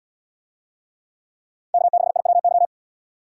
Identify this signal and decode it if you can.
Morse code, heard as a keyed tone, about 72 wpm, 700 Hz. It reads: K6IWY